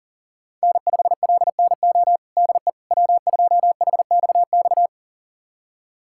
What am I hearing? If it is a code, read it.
Morse code, heard as a keyed tone, about 40 wpm, 700 Hz. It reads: N5LNO DE W2HXX
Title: USA Callsign DE Callsign - 5 Characters - 40wpm